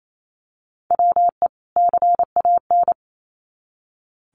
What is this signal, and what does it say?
Morse code, heard as a keyed tone, about 28 wpm, 700 Hz. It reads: WE CAN
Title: Sets of 2 Words 28wpm